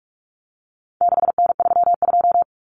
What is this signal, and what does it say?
Morse code, heard as a keyed tone, about 44 wpm, 700 Hz. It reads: 6N32
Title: Common Cut Numbers - 4-Digit 45wpm